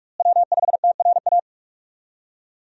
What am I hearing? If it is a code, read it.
Morse code, heard as a keyed tone, about 42 wpm, 700 Hz. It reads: W5TRA
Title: USA Call Signs - 5 Characters - Repeated 2x - 45wpm